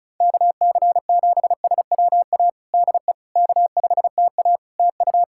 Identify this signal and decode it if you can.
Morse code, heard as a keyed tone, about 36 wpm, 700 Hz. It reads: KC7SWA DE K5TA TU
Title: Practice QSO - Exchange 4 - Ending - 35wpm